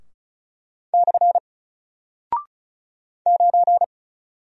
Morse code, heard as a keyed tone, about 36 wpm, 700 Hz. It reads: / 9
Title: Single Character (Letter:Number:Punctuation:Prosign) Alt 35wpm